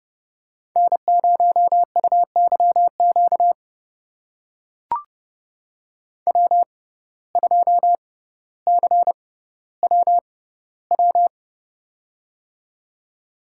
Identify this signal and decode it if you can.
Morse code, heard as a keyed tone, about 31 wpm, spaced as for 15 wpm, 700 Hz. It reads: N0UYQ W2CWW
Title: USA Call Signs - 5 Characters - 6x Character Spacing - 30wpm